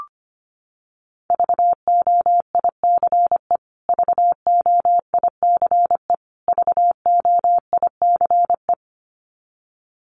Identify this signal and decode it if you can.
Morse code, heard as a keyed tone, about 25 wpm, 700 Hz. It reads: VOICE VOICE VOICE
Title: Top 500 Words Repeated 25wpm